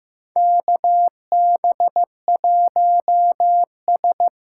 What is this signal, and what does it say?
Morse code, heard as a keyed tone, about 15 wpm, 700 Hz. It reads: KB1S
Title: USA Call Signs - 6 Characters - 6x Character Spacing - 15wpm